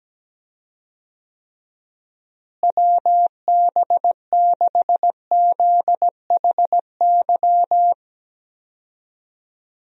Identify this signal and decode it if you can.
Morse code, heard as a keyed tone, about 17 wpm, 700 Hz. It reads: WB6ZHY
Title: USA Call Signs - 6 Characters - Repeated 3x - 17wpm